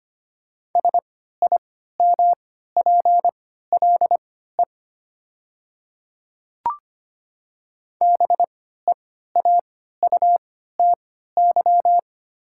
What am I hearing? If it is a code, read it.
Morse code, heard as a keyed tone, about 25 wpm, spaced as for 14 wpm, 690 Hz. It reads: SIMPLE BEAUTY
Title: 3x Character Spacing - Top 500 Words - 25wpm